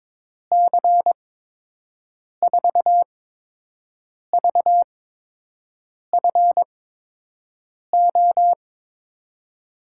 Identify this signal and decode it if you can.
Morse code, heard as a keyed tone, about 21 wpm, spaced as for 6 wpm, 700 Hz. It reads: C4VFO
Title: USA Call Signs - 8x Character Spacing - 22wpm